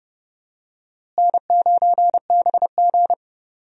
Morse code, heard as a keyed tone, about 28 wpm, 700 Hz. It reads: N9BG
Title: USA Call Signs - 4 Characters - Repeated 3x - 30wpm